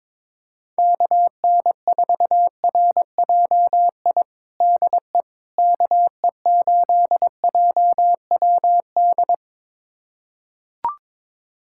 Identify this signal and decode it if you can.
Morse code, heard as a keyed tone, about 22 wpm, 700 Hz. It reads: KN4RJI DE KE8JWD
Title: USA Callsign DE Callsign - 6 Characters - 22wpm